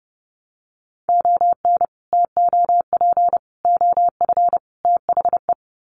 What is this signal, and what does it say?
Morse code, heard as a keyed tone, about 30 wpm, 700 Hz. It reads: ON TOP OF THE
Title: Sets of 4 Words 30wpm